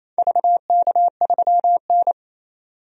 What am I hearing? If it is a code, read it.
Morse code, heard as a keyed tone, about 28 wpm, 700 Hz. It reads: VK3N
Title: DXCC-100 Call Signs - 4 Characters - 28wpm